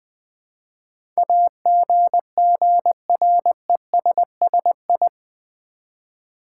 Morse code, heard as a keyed tone, about 20 wpm, 700 Hz. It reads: AGGRESSI
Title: Build A Word - 4 Letters At A Time - 10-Letter Words - 20wpm